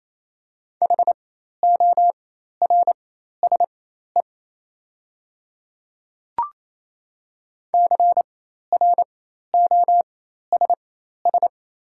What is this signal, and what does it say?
Morse code, heard as a keyed tone, about 27 wpm, spaced as for 13 wpm, 700 Hz. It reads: HORSE CROSS
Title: 4x Character Spacing - Top 300 Words - 28wpm